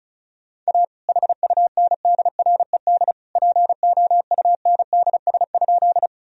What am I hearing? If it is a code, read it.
Morse code, heard as a keyed tone, about 34 wpm, 700 Hz. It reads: A HUNDRED POUNDS?
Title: Sentences from Top 1000 Words - Review Part 2 - 35wpm